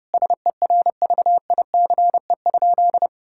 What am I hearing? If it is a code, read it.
Morse code, heard as a keyed tone, about 30 wpm, 700 Hz. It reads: SERVICE?
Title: Sentences from Top 1000 Words - Review Part 6 - 30wpm